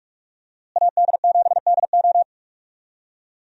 Morse code, heard as a keyed tone, about 46 wpm, 700 Hz. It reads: AD7DO
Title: USA Call Signs - 5 Characters - Repeated 3x - 45wpm